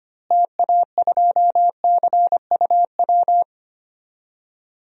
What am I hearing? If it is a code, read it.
Morse code, heard as a keyed tone, about 25 wpm, 700 Hz. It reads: TA2CUW
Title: DXCC-100 Call Signs - 6 Characters - Repeated 2x - 25wpm